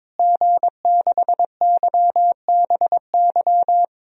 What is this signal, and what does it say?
Morse code, heard as a keyed tone, about 22 wpm, 700 Hz. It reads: G6YBY
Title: DXCC-100 Call Signs - 5 Characters - Repeated 2x - 22wpm